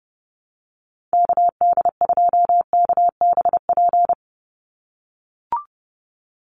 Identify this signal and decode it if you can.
Morse code, heard as a keyed tone, about 30 wpm, 700 Hz. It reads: KD2KBP
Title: USA Call Signs 30wpm